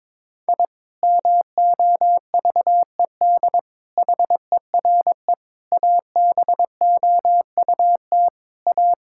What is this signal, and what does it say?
Morse code, heard as a keyed tone, about 22 wpm, 700 Hz. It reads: I MOVED HERE ABOUT A
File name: Sentences with 7-Word Phrases - 22wpm